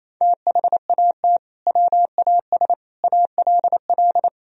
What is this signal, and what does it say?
Morse code, heard as a keyed tone, about 28 wpm, 700 Hz. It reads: THAT WAS ALL
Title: Sentences from Top 100 Words and Abbreviations 28wpm